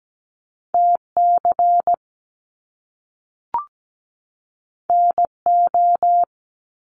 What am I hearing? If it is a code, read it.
Morse code, heard as a keyed tone, about 17 wpm, 700 Hz. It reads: TC NO
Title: Top Two Letter Combinations in Words 17wpm